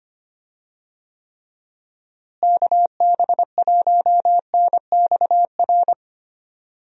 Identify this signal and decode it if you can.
Morse code, heard as a keyed tone, about 25 wpm, 700 Hz. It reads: KB1NXR
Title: USA Call Signs - 6 Characters - Repeated 3x - 25wpm